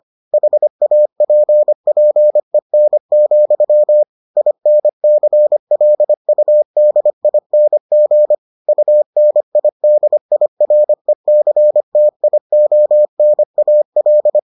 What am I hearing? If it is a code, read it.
Morse code, heard as a keyed tone, about 25 wpm, 600 Hz. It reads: HAPPEN, INCLUDING UNIDIRECTIONAL